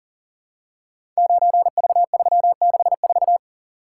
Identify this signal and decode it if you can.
Morse code, heard as a keyed tone, about 43 wpm, 700 Hz. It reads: 9V364
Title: Code Groups - 5 Letter:Numbers - 40wpm